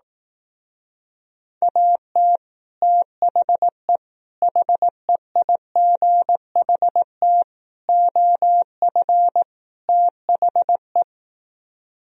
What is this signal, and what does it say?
Morse code, heard as a keyed tone, about 18 wpm, 700 Hz. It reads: AT THE HEIGHT OF THE
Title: Speed-Racing - Sets of 5 Words 15wpm